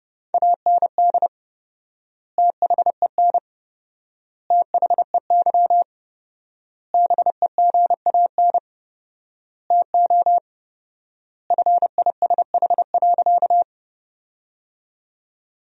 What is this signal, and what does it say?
Morse code, heard as a keyed tone, about 30 wpm, 700 Hz. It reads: AND THEN THEY BEGAN TO FISH.
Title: 4x Word-Spacing - Sentences from Top 300 Words - 30wpm